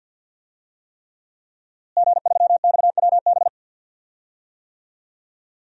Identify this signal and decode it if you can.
Morse code, heard as a keyed tone, about 52 wpm, 700 Hz. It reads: G3XWB
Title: DXCC-100 Call Signs - 5 Characters - Repeated 3x - 50wpm